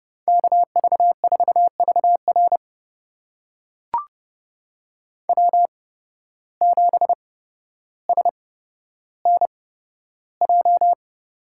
Morse code, heard as a keyed tone, about 30 wpm, spaced as for 13 wpm, 700 Hz. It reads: KV4VR W7SNJ
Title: USA Call Signs - 5 Characters - 8x Character Spacing - 30wpm